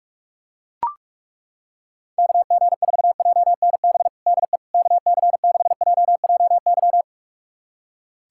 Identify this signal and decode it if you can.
Morse code, heard as a keyed tone, about 42 wpm, 700 Hz. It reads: KG4JNB DE KC6JJY
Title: USA Callsign DE Callsign 45wpm